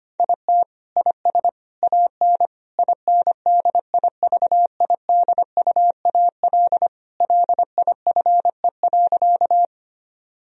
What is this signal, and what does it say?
Morse code, heard as a keyed tone, about 25 wpm, 700 Hz. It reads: IT IS AN INDIVIDUAL LIFE.
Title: Sentences from Top 1000 Words - Review Part 4 - 25wpm